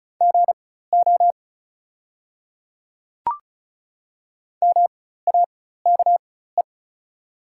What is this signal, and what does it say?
Morse code, heard as a keyed tone, about 34 wpm, spaced as for 16 wpm, 700 Hz. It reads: GO MAKE
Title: 4x Character Spacing - Top 200 Words - 35wpm